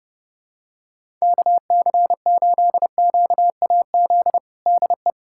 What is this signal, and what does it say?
Morse code, heard as a keyed tone, about 32 wpm, 700 Hz. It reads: KC8QAZ DE
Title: USA Callsign DE Callsign - 6 Characters - 30wpm